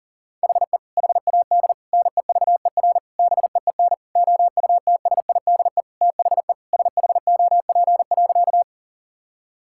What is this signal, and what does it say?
Morse code, heard as a keyed tone, about 37 wpm, 700 Hz. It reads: HE HAD NEVER BEEN OUTSIDE THE SHOP.
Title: Sentences from Top 1600 Words 40wpm